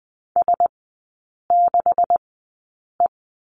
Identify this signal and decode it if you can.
Morse code, heard as a keyed tone, about 20 wpm, spaced as for 8 wpm, 700 Hz. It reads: S6E
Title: 166 Lesson 27e - Number 7 - Three Characters 20wpm